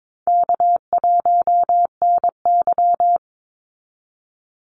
Morse code, heard as a keyed tone, about 22 wpm, 700 Hz. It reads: K1NY
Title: Easy Call Signs Repeated 3x - 22wpm